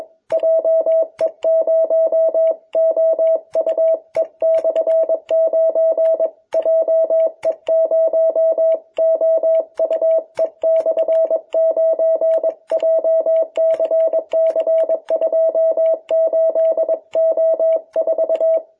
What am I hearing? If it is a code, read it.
Morse code, heard as a keyed tone, about 21 wpm, 660 Hz. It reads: JE0OUE/9 JE0OUE/9 JCC28O4
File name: jh8cbh / 2021cw